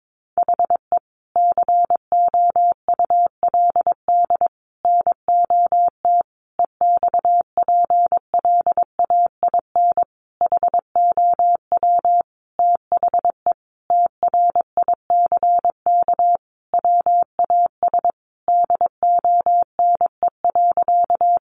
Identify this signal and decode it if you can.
Morse code, heard as a keyed tone, about 22 wpm, 700 Hz. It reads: HE COULD NOT EXPLAIN HOW THE TRICK WAS DONE.
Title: Sentences from Top 1500 Words 22wpm